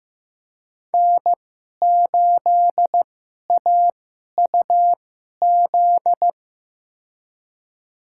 Morse code, heard as a keyed tone, about 15 wpm, spaced as for 11 wpm, 700 Hz. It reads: N8AUZ
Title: USA Call Signs - 5 Characters - 2x Character Spacing - 15wpm